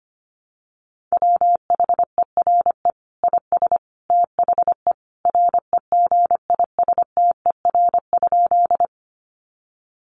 Morse code, heard as a keyed tone, about 25 wpm, 700 Hz. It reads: WHERE IS THE REGISTER?
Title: Sentences from Top 1200 Words 25wpm